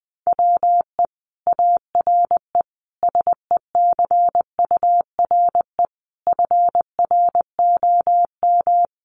Morse code, heard as a keyed tone, about 19 wpm, 700 Hz. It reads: WE ARE SECURE FROM
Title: Sentences from Top 1200 Words 20wpm